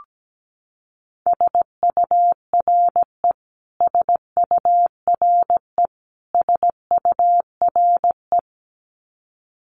Morse code, heard as a keyed tone, about 17 wpm, 700 Hz. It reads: SURE SURE SURE
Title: Top 500 Words Repeated 17wpm